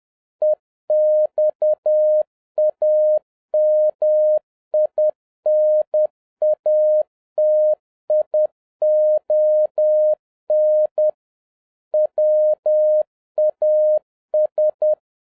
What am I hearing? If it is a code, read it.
Morse code, heard as a keyed tone, about 10 wpm, 610 Hz. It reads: EXAMINATION WAS